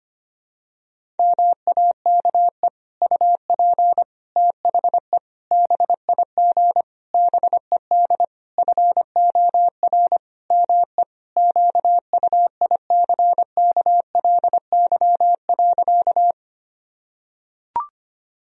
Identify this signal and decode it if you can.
Morse code, heard as a keyed tone, about 25 wpm, 700 Hz. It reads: MAKE UP THE BIG BED FOR ME QUICKLY.
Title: Sentences from Top 1000 Words - Review Part 1 - 25wpm